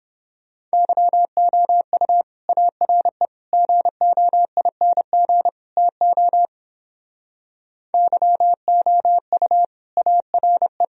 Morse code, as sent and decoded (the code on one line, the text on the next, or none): -.-- --- ..- / .- .-. / --. --- .. -. --. / - --- / -.-- --- ..- / .- .-. .
YOU ARE GOING TO YOU ARE